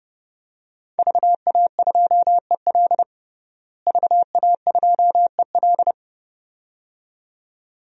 Morse code, heard as a keyed tone, about 30 wpm, 700 Hz. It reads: VA2EL VA2EL
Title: DXCC-100 Call Signs - 5 Characters - Repeated 2x - 30wpm